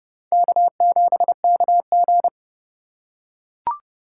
Morse code, heard as a keyed tone, about 30 wpm, 700 Hz. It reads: K7KG
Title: USA Call Signs - 4x Character Spacing - 30wpm